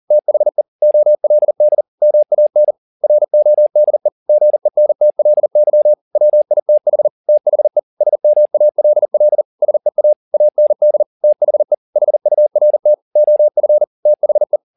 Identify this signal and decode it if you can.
Morse code, heard as a keyed tone, about 40 wpm, 600 Hz. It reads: THE OLD MAN RODE GENTLY WITH THE SMALL SEA AND THE HURT OF THE